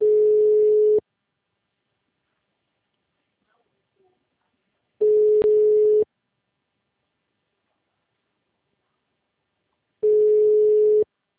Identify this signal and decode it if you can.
Ringback tone (Europe, 425 Hz).